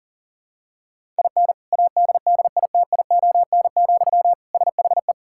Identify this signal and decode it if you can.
Morse code, heard as a keyed tone, about 40 wpm, 700 Hz. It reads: IN ADDITION, SHE
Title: Sentences from Top 1000 Words - Review Part 6 - 40wpm